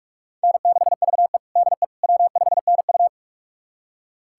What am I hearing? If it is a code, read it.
Morse code, heard as a keyed tone, about 41 wpm, 700 Hz. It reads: N6VE DE W5NU
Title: USA Callsign DE Callsign - 4 Characters - 45wpm